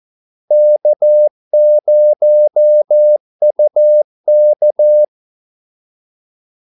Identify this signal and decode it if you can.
Morse code, heard as a keyed tone, about 14 wpm, 600 Hz. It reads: K0UK
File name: Callsigns-14wpm